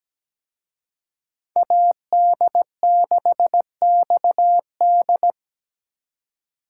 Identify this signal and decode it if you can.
Morse code, heard as a keyed tone, about 20 wpm, 700 Hz. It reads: AD6XD
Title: Speed Racing - USA 2x2 Call Signs 17wpm